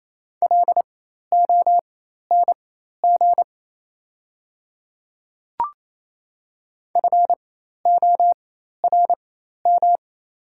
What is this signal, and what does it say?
Morse code, heard as a keyed tone, about 28 wpm, spaced as for 13 wpm, 700 Hz. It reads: LONG FORM
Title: 4x Character Spacing - Top 200 Words - 28wpm